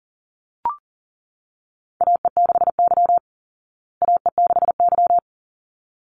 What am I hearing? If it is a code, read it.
Morse code, heard as a keyed tone, about 40 wpm, 700 Hz. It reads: AE6Y AE6Y
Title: Call Signs Repeated 3x - 40wpm